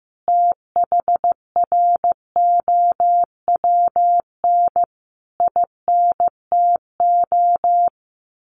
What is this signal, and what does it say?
Morse code, heard as a keyed tone, about 15 wpm, 700 Hz. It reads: THROWN INTO
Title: A Sound Of Thunder 15wpm